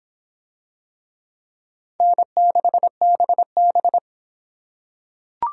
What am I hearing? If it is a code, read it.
Morse code, heard as a keyed tone, about 25 wpm, 700 Hz. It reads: N6BB